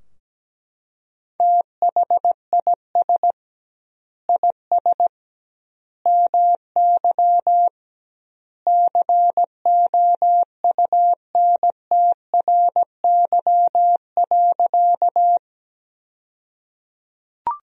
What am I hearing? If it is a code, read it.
Morse code, heard as a keyed tone, about 17 wpm, 700 Hz. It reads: THIS IS MY COUNTRY.
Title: 2x Word-Spacing - Sentences from Top 300 Words - 17wpm